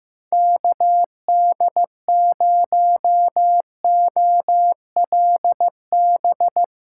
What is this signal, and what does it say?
Morse code, heard as a keyed tone, about 16 wpm, 700 Hz. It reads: KD0OLB
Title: USA Callsign DE Callsign 15wpm